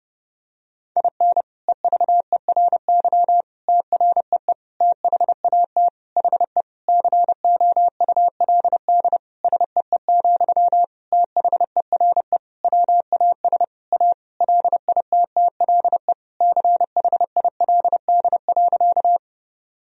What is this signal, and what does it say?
Morse code, heard as a keyed tone, about 30 wpm, 700 Hz. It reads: IN EVERY TREE THAT HE COULD SEE, THERE WAS A LITTLE CHILD.
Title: Sentences from Top 1000 Words - Review Part 1 - 30wpm